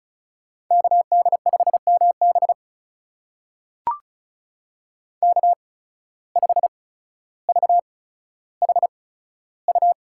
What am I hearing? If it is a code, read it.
Morse code, heard as a keyed tone, about 36 wpm, spaced as for 15 wpm, 700 Hz. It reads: KD5MB K5VHU
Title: USA Call Signs - 5 Characters - 8x Character Spacing - 35wpm